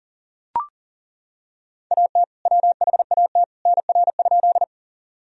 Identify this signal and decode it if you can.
Morse code, heard as a keyed tone, about 37 wpm, 700 Hz. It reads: AT WHAT NR?
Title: Sentences from Top 100 Words and Abbreviations 40wpm